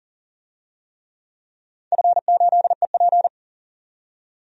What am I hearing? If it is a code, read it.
Morse code, heard as a keyed tone, about 40 wpm, 700 Hz. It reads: F8EP
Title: DXCC-100 Call Signs - 4 Characters - 40wpm